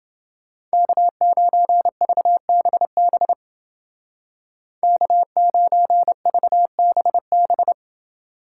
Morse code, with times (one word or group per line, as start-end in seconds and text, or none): 0.7-3.3 K9VBB
4.8-7.7 K9VBB